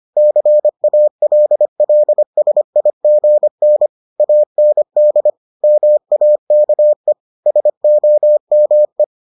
Morse code, heard as a keyed tone, about 25 wpm, 600 Hz. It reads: CALLSIGN AND MAKE SOME